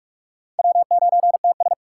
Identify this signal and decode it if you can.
Morse code, heard as a keyed tone, about 58 wpm, 700 Hz. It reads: W9TS